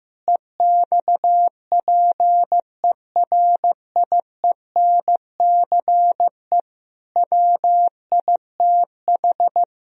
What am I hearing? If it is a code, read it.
Morse code, heard as a keyed tone, about 15 wpm, 700 Hz. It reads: EXPERIENCE WITH
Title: Sentences from Top 1000 Words - Review Part 3 - 15wpm